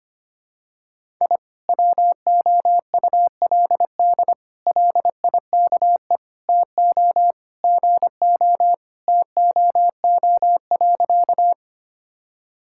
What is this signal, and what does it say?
Morse code, heard as a keyed tone, about 25 wpm, 700 Hz. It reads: I WOULD LIKE TO GO TOO.
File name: practice / 25wpm / Sentences with 3-Word Phrases - 25wpm